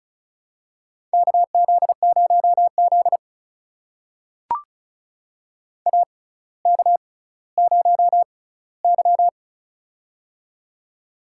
Morse code, heard as a keyed tone, about 36 wpm, spaced as for 18 wpm, 700 Hz. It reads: KZ0Z AK0Y
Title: USA Call Signs - 4 Characters - 6x Character Spacing - 35wpm